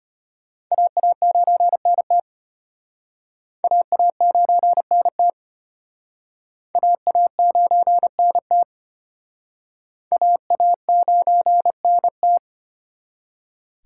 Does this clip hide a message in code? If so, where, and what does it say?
Yes: Morse code from 0.7 to 12.4 s, heard as a keyed tone, about 31 wpm, 700 Hz. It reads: AA9NT AA9NT AA9NT AA9NT